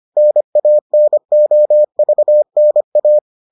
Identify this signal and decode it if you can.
Morse code, heard as a keyed tone, about 25 wpm, 600 Hz. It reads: NANOVNA